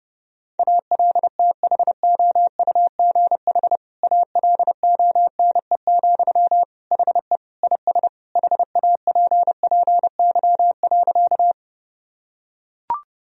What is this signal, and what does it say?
Morse code, heard as a keyed tone, about 32 wpm, 700 Hz. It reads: ALTHOUGH ALONE, HE IS HAPPY.
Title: Sentences from Top 1000 Words - Review Part 3 - 30wpm